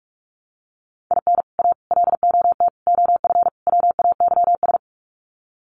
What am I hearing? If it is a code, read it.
Morse code, heard as a keyed tone, about 44 wpm, 700 Hz. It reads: IN A LOT OF WAYS
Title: Sets of 5 Words Encore 45wpm